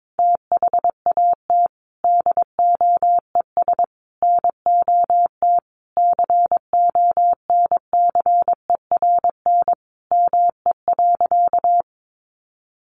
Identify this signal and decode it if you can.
Morse code, heard as a keyed tone, about 22 wpm, 700 Hz. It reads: THAT DOES NOT CONCERN ME.
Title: Sentences from Top 1000 Words 22wpm